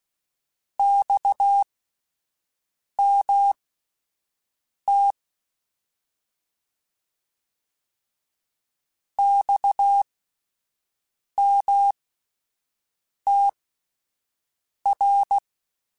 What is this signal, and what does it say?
Morse code, heard as a keyed tone, about 16 wpm, spaced as for 5 wpm, 800 Hz. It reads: XMT XMTR
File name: Abreviations / Abreviations mp3s / Abreviations@5